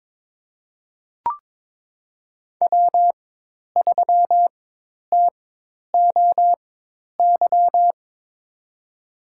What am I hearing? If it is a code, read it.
Morse code, heard as a keyed tone, about 21 wpm, spaced as for 10 wpm, 700 Hz. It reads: W3TOY
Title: USA Call Signs - 4x Character Spacing - 22wpm